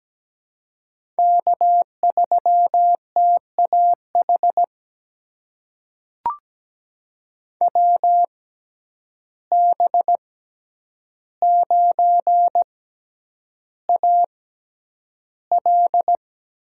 Morse code, heard as a keyed tone, about 17 wpm, spaced as for 9 wpm, 700 Hz. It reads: K3TAH WB9AL
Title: USA Call Signs - 5 Characters - 6x Character Spacing - 17wpm